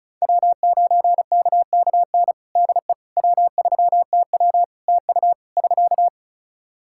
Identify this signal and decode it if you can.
Morse code, heard as a keyed tone, about 36 wpm, 700 Hz. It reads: W9KKN DE W3TW TU <SK>
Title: Practice QSO - Exchange 4 - Ending - 35wpm